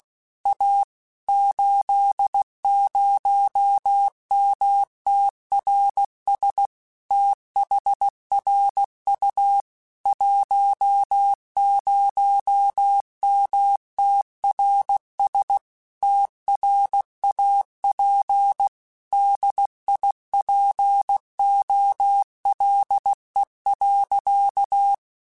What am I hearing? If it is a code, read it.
Morse code, heard as a keyed tone, about 16 wpm, 800 Hz. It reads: A 80MTRS THRU 10MTRS TRAP DIPOLE.
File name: QSOs / QSOs-mp3s / WD0FBT 1@15